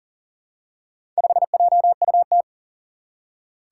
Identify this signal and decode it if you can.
Morse code, heard as a keyed tone, about 40 wpm, 700 Hz. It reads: 5JUT
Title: Four Challenging Letter-Number Combinations 40wpm